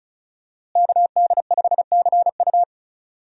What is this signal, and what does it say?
Morse code, heard as a keyed tone, about 36 wpm, 700 Hz. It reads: KD5CU
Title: USA Call Signs - 5 Characters - Repeated 3x - 35wpm